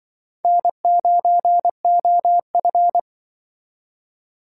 Morse code, heard as a keyed tone, about 24 wpm, 700 Hz. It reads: N9OF
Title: Speed Racing - USA Call Signs 20wpm